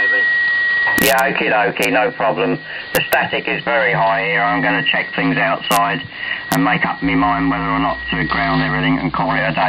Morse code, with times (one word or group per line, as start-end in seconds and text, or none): none